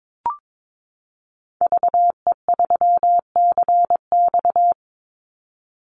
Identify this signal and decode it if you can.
Morse code, heard as a keyed tone, about 22 wpm, 700 Hz. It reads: VE3CX
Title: Call Signs - 22wpm